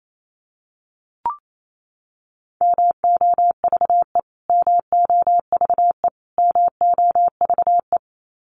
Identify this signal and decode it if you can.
Morse code, heard as a keyed tone, about 28 wpm, 700 Hz. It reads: MOVE MOVE MOVE
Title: Top 200 Words Repeated 28wpm